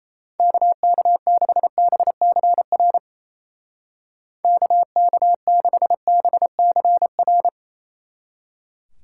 Morse code, heard as a keyed tone, about 34 wpm, 700 Hz. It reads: KK6BCR KK6BCR
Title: Speed Racing - USA 2x3 Call Signs 28wpm